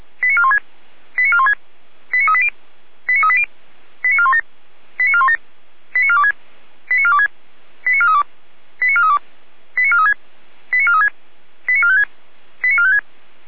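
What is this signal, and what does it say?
Five-tone selective call (ZVEI), heard as quick runs of five tones, tones of about 70 ms. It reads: 89416 89416 89380 89380 89417 89417 89426 89426 89432 89432 89436 89436 89456 89456
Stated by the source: ZVEI five-tone selective call, code 89192, received with an SDR.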